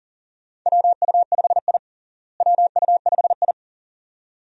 Morse code, heard as a keyed tone, about 40 wpm, 700 Hz. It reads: WU5I WU5I
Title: USA Call Signs - 4 Characters - Repeated 2x - 40wpm